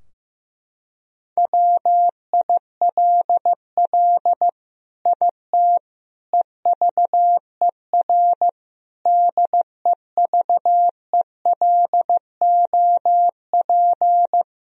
Morse code, heard as a keyed tone, about 14 wpm, 700 Hz. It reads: WILL IT EVER DEVELOP
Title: Sentences from Top 1000 Words - Review Part 2 - 15wpm